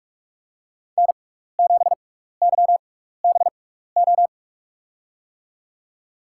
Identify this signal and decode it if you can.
Morse code, heard as a keyed tone, about 48 wpm, spaced as for 16 wpm, 700 Hz. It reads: N7YBO
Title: USA Call Signs - 5 Characters - 6x Character Spacing - 45wpm